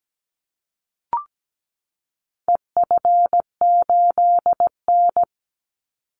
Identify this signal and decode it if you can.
Morse code, heard as a keyed tone, about 17 wpm, 700 Hz. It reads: EF8N
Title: Call Signs Repeated 3x - 17wpm